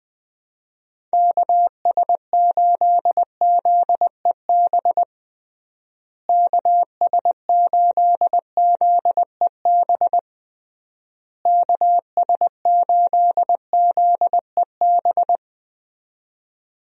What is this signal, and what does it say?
Morse code, heard as a keyed tone, about 21 wpm, 700 Hz. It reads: KS8ZEB KS8ZEB KS8ZEB
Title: USA Call Signs - 6 Characters - Repeated 3x - 20wpm